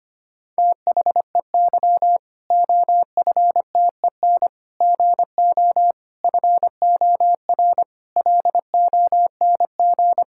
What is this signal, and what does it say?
Morse code, heard as a keyed tone, about 25 wpm, 700 Hz. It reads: THEY OFTEN GO FOR LONG
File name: practice / 25wpm / Difficult Sentences with Hints - 25wpm